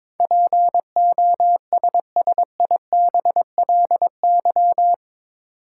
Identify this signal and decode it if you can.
Morse code, heard as a keyed tone, about 22 wpm, 700 Hz. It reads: POSSIBLY
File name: practice / 22wpm / Eight Letter Words - 22wpm